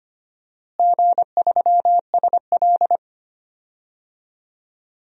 Morse code, heard as a keyed tone, about 25 wpm, 700 Hz. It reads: G3SL
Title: DXCC-100 Call Signs - 4 Characters - Repeated 2x - 25wpm